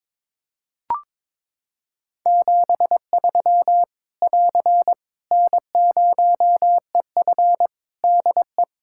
Morse code, heard as a keyed tone, about 22 wpm, 700 Hz. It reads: 73 <AR> N0EF DE
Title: Practice QSO - Exchange 4 - Ending - 22wpm